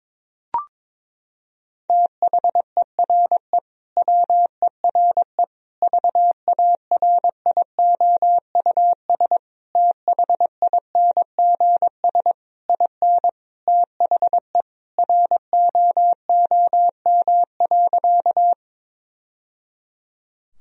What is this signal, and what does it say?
Morse code, heard as a keyed tone, about 21 wpm, 700 Hz. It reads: THERE WERE VARIOUS THINGS IN THE ROOM.
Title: Sentences from Top 1000 Words - Review Part 4 - 22wpm